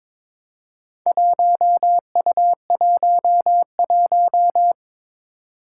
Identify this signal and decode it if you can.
Morse code, heard as a keyed tone, about 22 wpm, 700 Hz. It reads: 1U11
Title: Four Challenging Letter-Number Combinations 22wpm